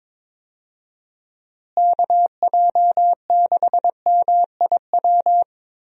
Morse code, heard as a keyed tone, about 22 wpm, 700 Hz. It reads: KJ6MIW
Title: USA Call Signs - 6 Characters - Repeated 2x - 22wpm